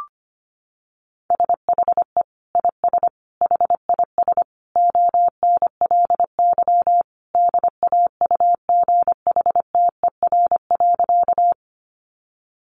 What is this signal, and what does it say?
Morse code, heard as a keyed tone, about 25 wpm, 700 Hz. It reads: SHE IS HIS ONLY DAUGHTER.
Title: Sentences from Top 1500 Words 25wpm